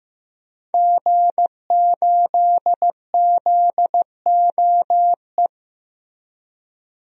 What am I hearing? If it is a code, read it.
Morse code, heard as a keyed tone, about 15 wpm, 700 Hz. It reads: G8ZOE
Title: DXCC-100 Call Signs - 5 Characters - Repeated 2x - 15wpm